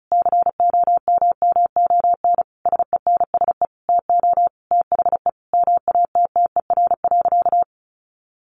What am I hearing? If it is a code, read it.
Morse code, heard as a keyed tone, about 34 wpm, 700 Hz. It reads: COMMON SENSE TO THE MATTER.
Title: Sentences from Top 1200 Words 35wpm